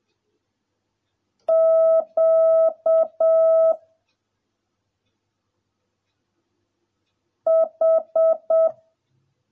Morse code, heard as a keyed tone, about 7 wpm, 650 Hz. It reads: Q H